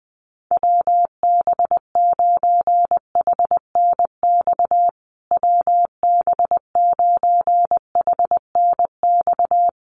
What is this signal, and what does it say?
Morse code, heard as a keyed tone, about 20 wpm, 700 Hz. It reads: WB9HNX WB9HNX
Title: USA Call Signs Repeated 3x 20wpm